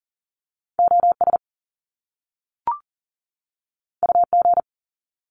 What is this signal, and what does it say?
Morse code, heard as a keyed tone, about 40 wpm, 700 Hz. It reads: OS UG
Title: Top Two Letter Combinations in Words 40wpm